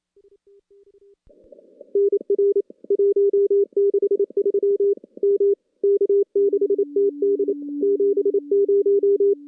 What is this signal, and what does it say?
Morse code, heard as a keyed tone, about 28 wpm, 400 Hz. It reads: NR 163 M K6TD 70
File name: contests / CW-SS-k5tr- -2300